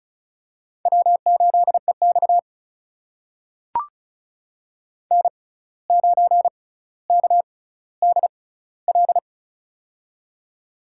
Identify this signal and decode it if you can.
Morse code, heard as a keyed tone, about 36 wpm, spaced as for 17 wpm, 700 Hz. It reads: W8EX N9KDL
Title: USA Call Signs - 6x Character Spacing - 35wpm